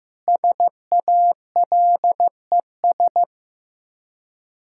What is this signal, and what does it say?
Morse code, heard as a keyed tone, about 15 wpm, 700 Hz. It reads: SALES
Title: Practice QSO - Exchange 3 - More Info - 15wpm